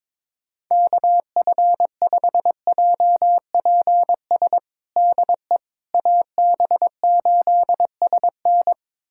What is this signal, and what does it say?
Morse code, heard as a keyed tone, about 22 wpm, 700 Hz. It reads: KF5JPS DE AB8SN
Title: USA Callsign DE Callsign 22wpm